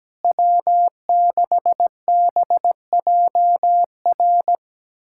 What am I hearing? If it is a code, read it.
Morse code, heard as a keyed tone, about 17 wpm, 700 Hz. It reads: W6BJR
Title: USA Callsign DE Callsign - 5 Characters - 17wpm